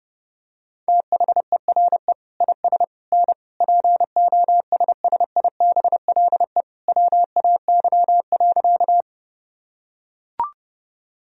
Morse code, heard as a keyed tone, about 31 wpm, 700 Hz. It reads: THERE IS N POSSIBLE WAY.
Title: Sentences from Top 500 Words and Abbreviations 30wpm